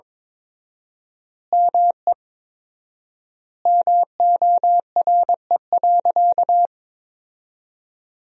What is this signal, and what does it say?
Morse code, heard as a keyed tone, about 22 wpm, 700 Hz. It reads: ME MORE.
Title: 4x Word-Spacing - Sentences from Top 300 Words - 22wpm